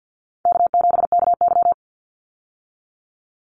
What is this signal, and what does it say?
Morse code, heard as a keyed tone, about 49 wpm, 700 Hz. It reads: K7KY
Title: Call Signs Repeated 3x - 50wpm